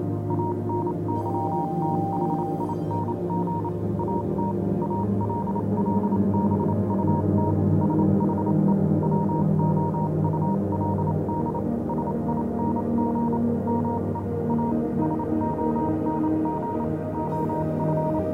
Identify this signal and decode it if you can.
Morse code, heard as a keyed tone, about 28 wpm, spaced as for 25 wpm, 940 Hz, with noise. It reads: ANYTHING AT ALL BUT SHE WOULD SING ME A SONG OF
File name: do you remember the song she sang (morse code version)